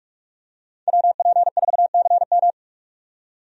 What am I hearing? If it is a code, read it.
Morse code, heard as a keyed tone, about 46 wpm, 700 Hz. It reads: WP4CM
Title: USA Call Signs - 5 Characters - 45wpm